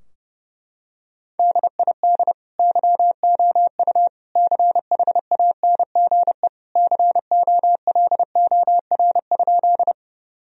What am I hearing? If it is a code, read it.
Morse code, heard as a keyed tone, about 30 wpm, 700 Hz. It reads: DID YOU CHANGE COLOR?
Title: Sentences from Top 1000 Words - Review Part 2 - 30wpm